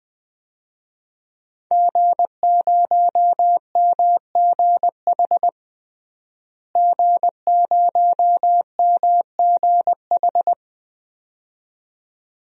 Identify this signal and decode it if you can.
Morse code, heard as a keyed tone, about 20 wpm, 700 Hz. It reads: G0MGH G0MGH
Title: DXCC-100 Call Signs - 5 Characters - Repeated 2x - 20wpm